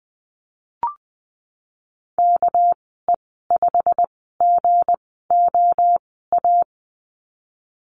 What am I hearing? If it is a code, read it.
Morse code, heard as a keyed tone, about 20 wpm, spaced as for 15 wpm, 700 Hz. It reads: KE5GOA
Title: 207 Lesson 33g - Letter j - Call Signs 20wpm